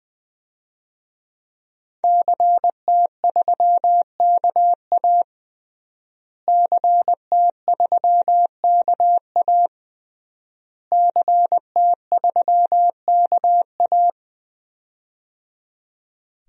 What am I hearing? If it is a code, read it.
Morse code, heard as a keyed tone, about 20 wpm, 700 Hz. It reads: CT3KA CT3KA CT3KA
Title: DXCC-100 Call Signs - 5 Characters - Repeated 3x - 20wpm